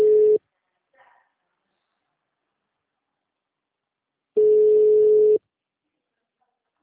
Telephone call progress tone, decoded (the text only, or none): ringback tone